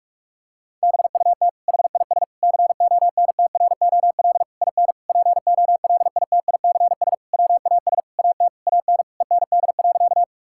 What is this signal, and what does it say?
Morse code, heard as a keyed tone, about 46 wpm, 700 Hz. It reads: BUT HIS CONTROL IN POLITICS WAS AT AN END.
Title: Sentences from Top 1000 Words - Review Part 6 - 45wpm